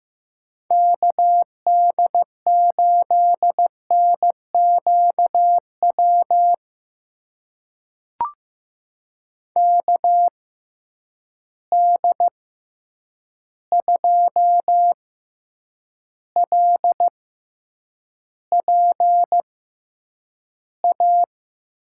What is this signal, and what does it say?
Morse code, heard as a keyed tone, about 15 wpm, spaced as for 8 wpm, 700 Hz. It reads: KD8NQW KD2LPA
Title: USA Call Signs - 6 Characters - 6x Character Spacing - 15wpm